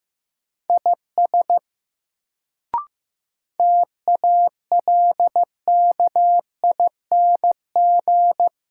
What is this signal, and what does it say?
Morse code, heard as a keyed tone, about 15 wpm, 700 Hz. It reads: IS TALKING
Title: Sets of 2 Words - New Word Drill - 15wpm